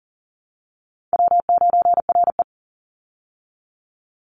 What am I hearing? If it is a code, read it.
Morse code, heard as a keyed tone, about 40 wpm, 700 Hz. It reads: W9RE
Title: Easy Call Signs - 40wpm